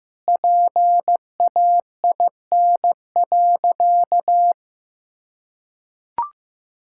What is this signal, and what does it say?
Morse code, heard as a keyed tone, about 16 wpm, 700 Hz. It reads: PAIN.
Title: Sentences from Top 1600 Words 15wpm